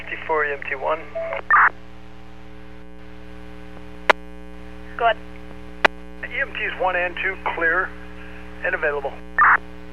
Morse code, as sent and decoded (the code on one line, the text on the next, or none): none